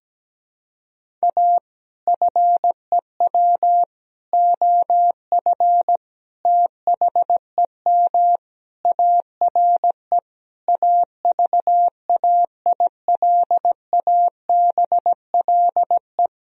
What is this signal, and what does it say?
Morse code, heard as a keyed tone, about 17 wpm, 700 Hz. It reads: A FEW OF THEM ARE AVAILABLE
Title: Sentences from Top 1000 Words - Review Part 3 - 17wpm